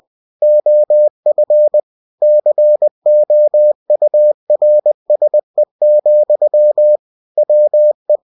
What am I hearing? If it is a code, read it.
Morse code, heard as a keyed tone, about 20 wpm, 600 Hz. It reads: OF COURSE, WE
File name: cw-books / winnie-the-pooh / winnie-20wpm-page-0114